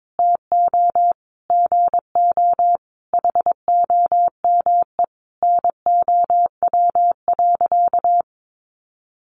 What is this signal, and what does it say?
Morse code, heard as a keyed tone, about 22 wpm, 700 Hz. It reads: TO GO HOME NOW.
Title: Sentences from Top 1300 Words 22wpm